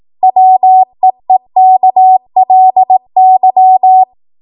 Morse code, heard as a keyed tone, about 18 wpm, 750 Hz. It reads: WEEKLY
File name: Morse / 18WPM